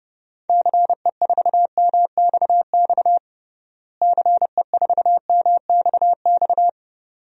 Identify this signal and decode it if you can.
Morse code, heard as a keyed tone, about 29 wpm, 700 Hz. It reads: CE4MXX CE4MXX
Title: DXCC-100 Call Signs - 6 Characters - Repeated 2x - 30wpm